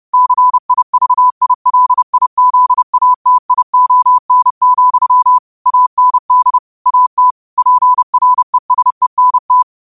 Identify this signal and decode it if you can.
Morse code, heard as a keyed tone, about 30 wpm, 1000 Hz. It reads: CIVILIZATION, AND AT PRESENT
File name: qotdfiles / 30wpm